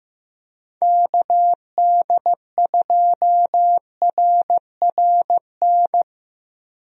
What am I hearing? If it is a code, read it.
Morse code, heard as a keyed tone, about 15 wpm, 700 Hz. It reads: KD2RRN
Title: USA Call Signs - 6 Characters - Repeated 3x - 15wpm